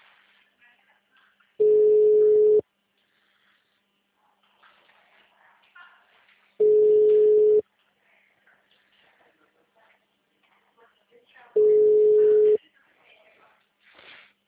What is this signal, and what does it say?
Ringback tone (Europe, 425 Hz), with speech.